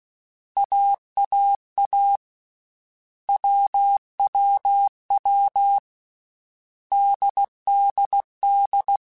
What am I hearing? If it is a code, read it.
Morse code, heard as a keyed tone, about 16 wpm, 800 Hz. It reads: AAA WWW DDD
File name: Basic Sets / Basic Letter Groups Collective7 / AWUNDG@13